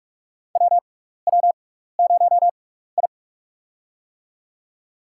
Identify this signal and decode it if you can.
Morse code, heard as a keyed tone, about 41 wpm, spaced as for 15 wpm, 700 Hz. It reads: WW0I
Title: USA Call Signs - 4 Characters - 6x Character Spacing - 45wpm